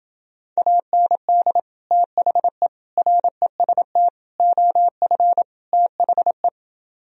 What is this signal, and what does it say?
Morse code, heard as a keyed tone, about 31 wpm, 700 Hz. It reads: AND THE REST OF THE